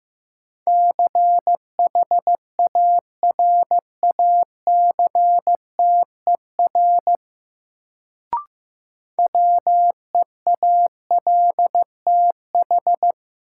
Morse code, heard as a keyed tone, about 15 wpm, 700 Hz. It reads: CHARACTER WEALTH
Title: Sentences from Top 1200 Words - New Word Drill Alt - 15wpm